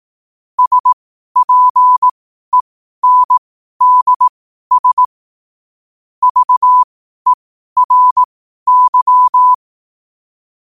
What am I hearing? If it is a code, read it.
Morse code, heard as a keyed tone, about 18 wpm, spaced as for 12 wpm, 1000 Hz. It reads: SPENDS VERY
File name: qotdfiles / 12wpm